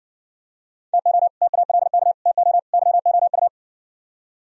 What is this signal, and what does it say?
Morse code, heard as a keyed tone, about 64 wpm, 700 Hz. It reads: TO TALK TO YOU